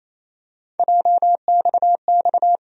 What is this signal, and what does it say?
Morse code, heard as a keyed tone, about 28 wpm, 700 Hz. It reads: JXX
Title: USA Call Sign Postfixes - 3 Letters 28wpm